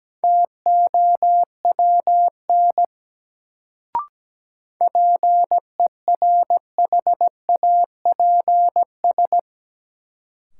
Morse code, heard as a keyed tone, about 17 wpm, 700 Hz. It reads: TOWN PERHAPS
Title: Sentences from Top 500 Words - New Word Drill - 17wpm